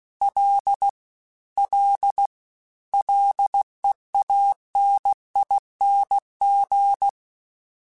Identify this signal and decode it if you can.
Morse code, heard as a keyed tone, about 16 wpm, 800 Hz. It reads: L L LEANING
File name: Combinations / Combinations-mp3s / Combinations@15